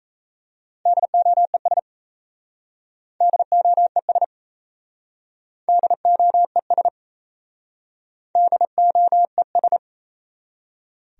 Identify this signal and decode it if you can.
Morse code, heard as a keyed tone, about 35 wpm, 700 Hz. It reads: DOES DOES DOES DOES